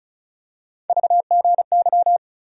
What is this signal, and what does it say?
Morse code, heard as a keyed tone, about 35 wpm, 700 Hz. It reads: VGY
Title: USA Call Sign Postfixes - 3 Letters 35wpm